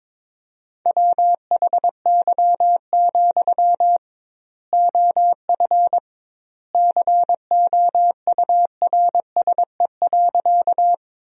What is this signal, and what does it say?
Morse code, heard as a keyed tone, about 22 wpm, 700 Hz. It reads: WHY, OF COURSE.
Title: 2x Word-Spacing - Sentences from Top 500 Words - 22wpm